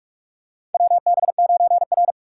Morse code, heard as a keyed tone, about 46 wpm, 700 Hz. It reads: WB9R
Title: USA Call Signs - 4 Characters - 4x Character Spacing - 45wpm